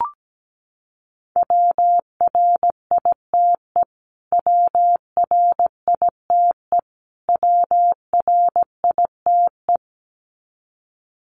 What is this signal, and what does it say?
Morse code, heard as a keyed tone, about 17 wpm, 700 Hz. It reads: WRITE WRITE WRITE
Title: Top 300 Words Repeated 17wpm